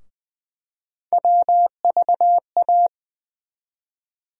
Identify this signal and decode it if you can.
Morse code, heard as a keyed tone, about 20 wpm, 700 Hz. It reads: WVA
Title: USA Call Sign Postfixes - 3 Letters 20wpm